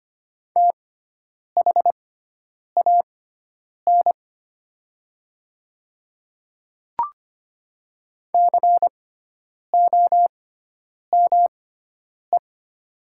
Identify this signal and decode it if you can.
Morse code, heard as a keyed tone, about 25 wpm, spaced as for 9 wpm, 700 Hz. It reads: THAN COME
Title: 6x Character Spacing - Top 100 Words - 25wpm